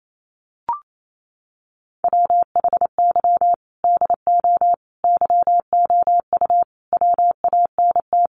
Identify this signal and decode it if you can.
Morse code, heard as a keyed tone, about 28 wpm, 700 Hz. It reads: WHY DO YOU WANT
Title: Sentences from Top 200 Words 28wpm